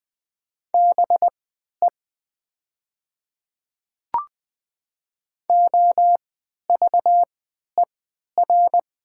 Morse code, heard as a keyed tone, about 20 wpm, spaced as for 11 wpm, 700 Hz. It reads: BE OVER
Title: 3x Character Spacing - Top 100 Words - 20wpm